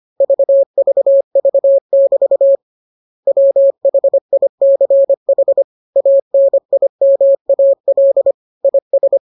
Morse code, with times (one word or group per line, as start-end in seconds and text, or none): 0.2-2.6 VVV<BT>
3.3-5.6 WHICH
6.0-8.3 ANIMAL
8.6-9.2 IS